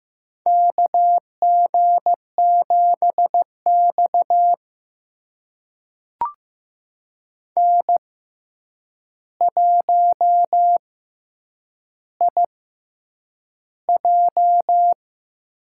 Morse code, heard as a keyed tone, about 15 wpm, spaced as for 8 wpm, 700 Hz. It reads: KG7X N1IJ
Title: USA Call Signs - 4 Characters - 6x Character Spacing - 15wpm